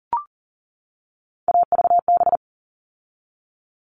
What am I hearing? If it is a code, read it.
Morse code, heard as a keyed tone, about 40 wpm, 700 Hz. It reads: AVB
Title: All Cut Numbers - 3-Digit 40wpm